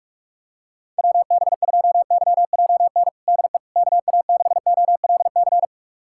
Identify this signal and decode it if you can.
Morse code, heard as a keyed tone, about 41 wpm, 700 Hz. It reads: WB2YJN DE KA6OLC
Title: USA Callsign DE Callsign - 6 Characters - 45wpm